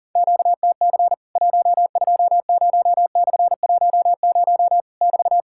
Morse code, heard as a keyed tone, about 40 wpm, 700 Hz. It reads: QTC 120/10 <BT>
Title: Worked All Europe DX Contest - QTC Exchange - 40wpm